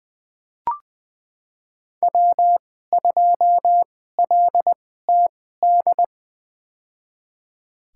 Morse code, heard as a keyed tone, about 20 wpm, spaced as for 14 wpm, 700 Hz. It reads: W2LTD
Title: USA Call Signs - 5 Characters - 2x Character Spacing - 20wpm